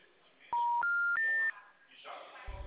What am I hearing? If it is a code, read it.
Special information tone.
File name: out- -2009- - - .760